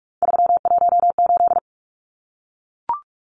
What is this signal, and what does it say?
Morse code, heard as a keyed tone, about 44 wpm, 700 Hz. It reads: 318
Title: Numbers - 3-Digit 45wpm